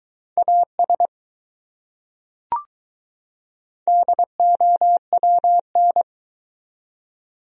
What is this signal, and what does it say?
Morse code, heard as a keyed tone, about 21 wpm, 700 Hz. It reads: AS DOWN